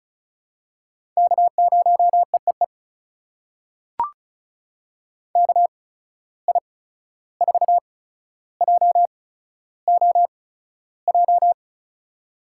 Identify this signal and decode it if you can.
Morse code, heard as a keyed tone, about 36 wpm, spaced as for 14 wpm, 700 Hz. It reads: K0EEE KI4JOJ
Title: USA Call Signs - 8x Character Spacing - 35wpm